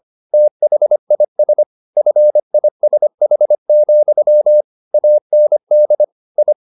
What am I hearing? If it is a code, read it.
Morse code, heard as a keyed tone, about 25 wpm, 600 Hz. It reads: THIS FISH, AND I
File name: the old man and the sea part 1 25wpm